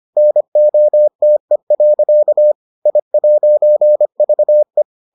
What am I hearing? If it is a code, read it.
Morse code, heard as a keyed tone, about 25 wpm, 600 Hz. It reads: NOTE. I'VE